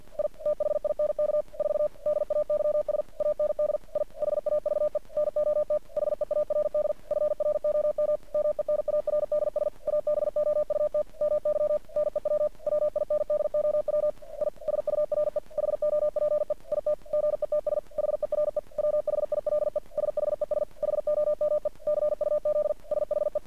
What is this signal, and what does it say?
Morse code, heard as a keyed tone, about 50 wpm, 600 Hz. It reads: I THINK 4 DAYS AND I HAVE NOT HEARD FROM MENARDS ABOUT MY NEW WINDOW I SURE HOPE IT GETS HERE WHILE SHES HOME CUZ SHE
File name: archived / CRSnet